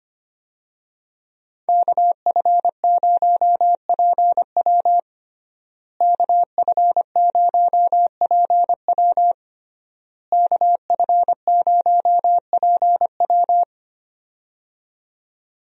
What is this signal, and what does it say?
Morse code, heard as a keyed tone, about 25 wpm, 700 Hz. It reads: KF0PW KF0PW KF0PW
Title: USA Call Signs - 5 Characters - Repeated 3x - 25wpm